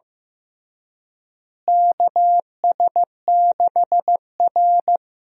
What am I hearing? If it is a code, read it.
Morse code, heard as a keyed tone, about 15 wpm, 700 Hz. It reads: KS6R
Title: USA Call Signs - 4 Characters - Repeated 2x - 15wpm